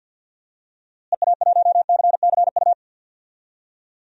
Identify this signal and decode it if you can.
Morse code, heard as a keyed tone, about 53 wpm, 700 Hz. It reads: EA1CCU
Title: DXCC-100 Call Signs - 6 Characters - Repeated 3x - 50wpm